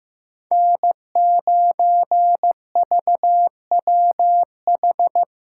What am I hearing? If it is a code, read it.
Morse code, heard as a keyed tone, about 15 wpm, 700 Hz. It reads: N9VWH
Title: USA Callsign DE Callsign - 5 Characters - 15wpm